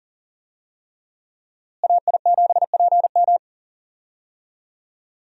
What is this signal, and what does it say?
Morse code, heard as a keyed tone, about 40 wpm, 700 Hz. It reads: AI7PM
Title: USA Call Signs - 5 Characters - Repeated 3x - 40wpm